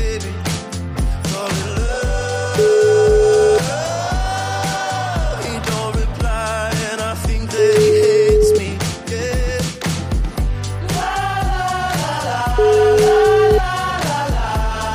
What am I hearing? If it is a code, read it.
Ringback tone (Europe, 425 Hz), with noise.